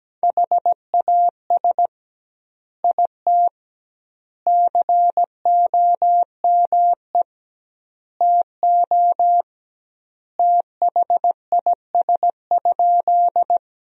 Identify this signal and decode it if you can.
Morse code, heard as a keyed tone, about 17 wpm, 700 Hz. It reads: HAS IT COME TO THIS?
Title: 2x Word-Spacing - Sentences from Top 100 Words - 17wpm